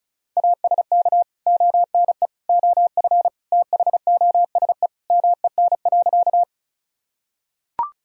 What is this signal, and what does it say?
Morse code, heard as a keyed tone, about 36 wpm, 700 Hz. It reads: ASK ONE OF THOSE MEN.
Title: Sentences from Top 1000 Words - Review Part 1 - 35wpm